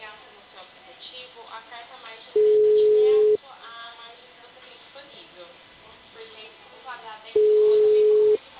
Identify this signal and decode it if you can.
Ringback tone (Europe, 425 Hz), with speech.
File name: out- -2004- - - .30501